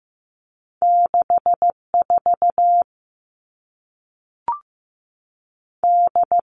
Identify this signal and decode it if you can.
Morse code, heard as a keyed tone, about 15 wpm, 700 Hz. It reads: 64 D
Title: All Cut Numbers - 2-Digit 15wpm